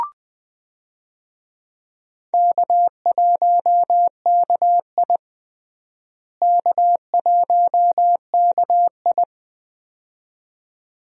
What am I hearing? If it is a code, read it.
Morse code, heard as a keyed tone, about 20 wpm, 700 Hz. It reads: K1KI K1KI
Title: USA Call Signs - 4 Characters - Repeated 2x - 20wpm